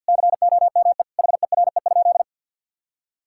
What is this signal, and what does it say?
Morse code, heard as a keyed tone, about 52 wpm, 700 Hz. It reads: COME HERE?
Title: Sentences from Top 1000 Words - Review Part 5 - 50wpm